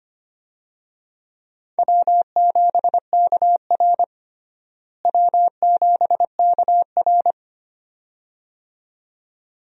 Morse code, heard as a keyed tone, about 25 wpm, 700 Hz. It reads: W7KR W7KR
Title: USA Call Signs - 4 Characters - Repeated 2x - 25wpm